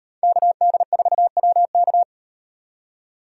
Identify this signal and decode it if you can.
Morse code, heard as a keyed tone, about 38 wpm, 700 Hz. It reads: KD4WK